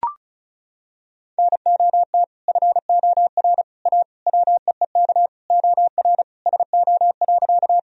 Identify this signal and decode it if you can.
Morse code, heard as a keyed tone, about 36 wpm, 700 Hz. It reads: NOT FOR A WEEK OR SO.
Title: Sentences from Top 1000 Words - Review Part 2 - 35wpm